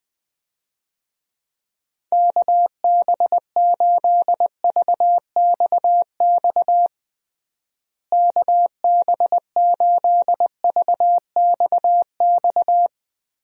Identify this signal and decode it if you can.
Morse code, heard as a keyed tone, about 20 wpm, 700 Hz. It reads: KB8VXX KB8VXX
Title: USA Call Signs - 6 Characters - Repeated 2x - 20wpm